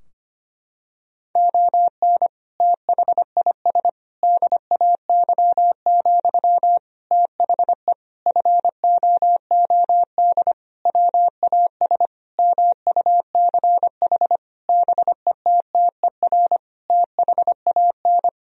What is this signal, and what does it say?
Morse code, heard as a keyed tone, about 25 wpm, 700 Hz. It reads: ON THIS DAY, THE FOOD WAS MUCH BETTER THAN
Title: Sentences from Top 1000 Words - Review Part 2 - 25wpm